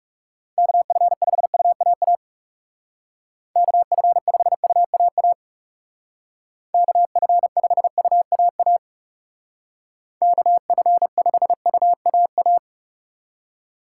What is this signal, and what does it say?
Morse code, heard as a keyed tone, about 35 wpm, 700 Hz. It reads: KF5UAA KF5UAA KF5UAA KF5UAA
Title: Speed Racing - USA 2x3 Call Signs 30wpm